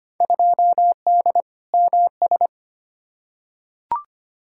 Morse code, heard as a keyed tone, about 25 wpm, 700 Hz. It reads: 2D MS
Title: Field Day Exchanges Encore - 25wpm